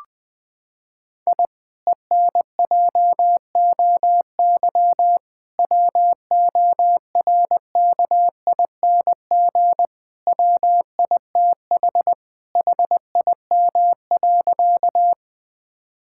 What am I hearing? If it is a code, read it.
Morse code, heard as a keyed tone, about 21 wpm, 700 Hz. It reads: I ENJOY WORKING WITH HIM.
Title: Sentences from Top 1000 Words - Review Part 5 - 20wpm